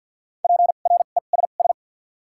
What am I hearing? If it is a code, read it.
Morse code, heard as a keyed tone, about 53 wpm, 700 Hz. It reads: P R E S S
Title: 2x Character Spacing - Top 300 Words - 50wpm